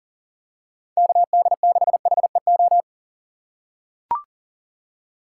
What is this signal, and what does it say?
Morse code, heard as a keyed tone, about 38 wpm, 690 Hz. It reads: KD6HEO
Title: USA Call Signs - 6 Characters - 2x Character Spacing - 40wpm